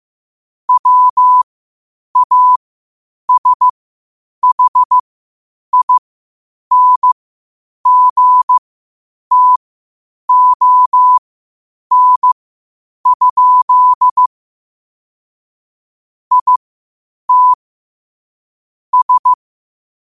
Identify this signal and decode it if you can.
Morse code, heard as a keyed tone, about 15 wpm, spaced as for 8 wpm, 1000 Hz. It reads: WASHINGTON? IT S
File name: qotdfiles / 8wpm